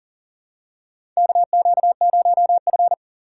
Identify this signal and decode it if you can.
Morse code, heard as a keyed tone, about 40 wpm, 700 Hz. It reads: KQ0F
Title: USA Call Signs - 4 Characters - 2x Character Spacing - 40wpm